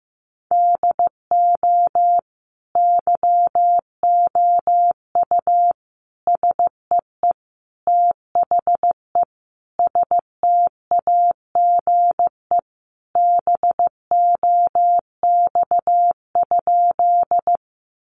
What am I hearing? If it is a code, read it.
Morse code, heard as a keyed tone, about 15 wpm, 700 Hz. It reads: DO YOU SEE THE STAGE BOX?
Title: Sentences from Top 1200 Words 15wpm